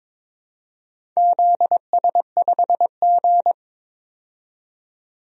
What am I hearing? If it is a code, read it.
Morse code, heard as a keyed tone, about 23 wpm, 700 Hz. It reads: ZS5G
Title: DXCC-100 Call Signs - 4 Characters - 22wpm